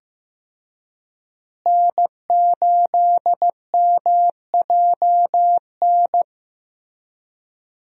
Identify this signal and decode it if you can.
Morse code, heard as a keyed tone, about 14 wpm, 700 Hz. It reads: N8MJN
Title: USA Call Signs - 5 Characters - Repeated 3x - 15wpm